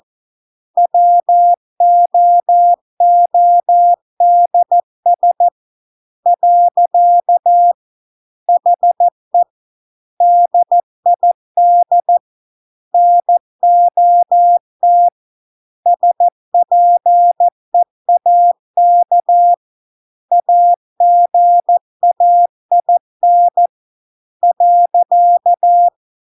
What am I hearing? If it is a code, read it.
Morse code, heard as a keyed tone, about 14 wpm, 700 Hz. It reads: WOODS . HE DID NOT SPEAK AGAIN .